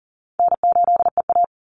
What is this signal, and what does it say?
Morse code, heard as a keyed tone, about 40 wpm, 700 Hz. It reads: N8EA
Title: Call Signs Repeated 2x - 40wpm